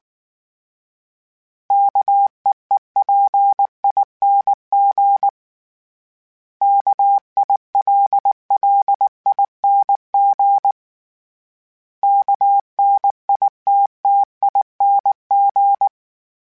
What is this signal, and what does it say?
Morse code, heard as a keyed tone, about 19 wpm, 800 Hz. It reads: KEEPING KILLING KNITTING
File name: Special INGs / Special INGs-mp3s / Sngl Ing @18